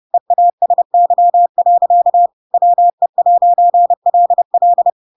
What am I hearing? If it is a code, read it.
Morse code, heard as a keyed tone, about 30 wpm, 700 Hz. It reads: EASY. WE'LL
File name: Pinocchio / Pinocchio ch8 30 30 ik1hge